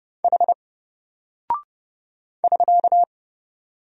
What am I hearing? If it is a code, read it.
Morse code, heard as a keyed tone, about 31 wpm, 700 Hz. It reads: H <SK>
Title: Single Character (Letter:Number:Punctuation:Prosign) 30wpm